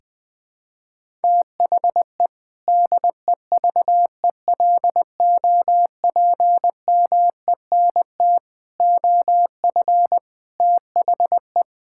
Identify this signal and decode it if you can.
Morse code, heard as a keyed tone, about 20 wpm, 700 Hz. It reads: THE DEVELOPMENT OF THE
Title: Sentences with 3-Word Phrases - Spoken Beginning - 20wpm